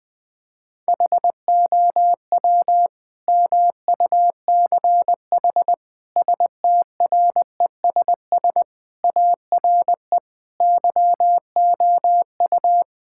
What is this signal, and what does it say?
Morse code, heard as a keyed tone, about 20 wpm, 700 Hz. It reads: HOW MUCH STRESS ARE YOU
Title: Sentences from Top 1000 Words - Review Part 6 - 20wpm